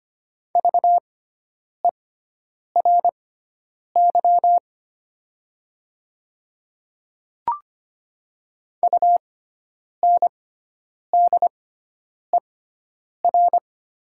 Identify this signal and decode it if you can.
Morse code, heard as a keyed tone, about 25 wpm, spaced as for 9 wpm, 700 Hz. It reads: VERY UNDER
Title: 6x Character Spacing - Top 200 Words - 25wpm